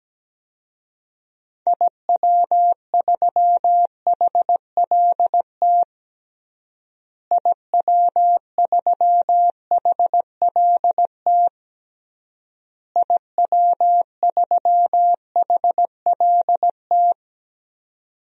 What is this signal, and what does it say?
Morse code, heard as a keyed tone, about 17 wpm, 700 Hz. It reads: IW3HLT IW3HLT IW3HLT
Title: DXCC-100 Call Signs - 6 Characters - Repeated 3x - 17wpm